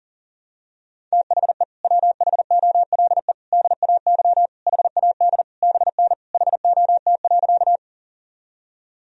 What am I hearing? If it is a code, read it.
Morse code, heard as a keyed tone, about 40 wpm, 700 Hz. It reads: THE WHOLE DAY HAD BN HOT.
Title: Sentences from Top 500 Words and Abbreviations 40wpm